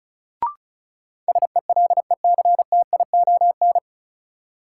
Morse code, heard as a keyed tone, about 36 wpm, 700 Hz. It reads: SELECTION
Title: Sentences from Top 1400 Words - New Word Drill - 35wpm